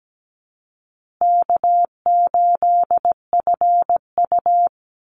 Morse code, heard as a keyed tone, about 17 wpm, 700 Hz. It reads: K8FU
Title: Easy Call Signs Repeated 3x - 17wpm